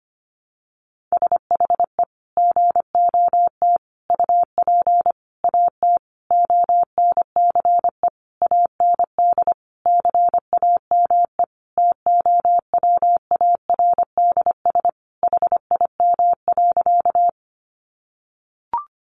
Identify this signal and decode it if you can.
Morse code, heard as a keyed tone, about 25 wpm, 700 Hz. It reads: SHE GOT UP AT ONCE AND CAME TOWARDS HIM.
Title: Sentences from Top 1100 Words 25wpm